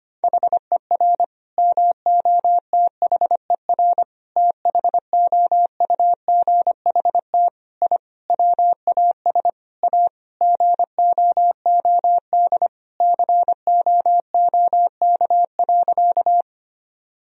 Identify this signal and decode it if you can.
Morse code, heard as a keyed tone, about 25 wpm, 700 Hz. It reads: HER MOTHER THOUGHT I WAS A GOOD COOK.
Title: Sentences from Top 1000 Words - Review Part 4 - 25wpm